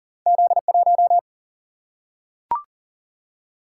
Morse code, heard as a keyed tone, about 38 wpm, 700 Hz. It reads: Z1
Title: Single-Letter Single-Number 40wpm